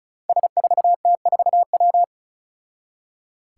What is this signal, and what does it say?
Morse code, heard as a keyed tone, about 36 wpm, 700 Hz. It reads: S4T4W
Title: Five Challenging Letter-Number Combinations 35wpm